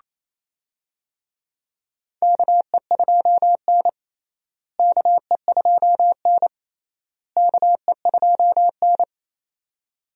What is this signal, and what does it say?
Morse code, heard as a keyed tone, about 27 wpm, 700 Hz. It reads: KE2N KE2N KE2N
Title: USA Call Signs - 4 Characters - Repeated 3x - 28wpm